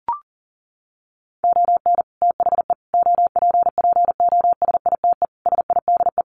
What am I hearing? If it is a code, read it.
Morse code, heard as a keyed tone, about 40 wpm, 700 Hz. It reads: ON THE OPPOSITE SIDE
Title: Sets of 5 Words Encore2 40wpm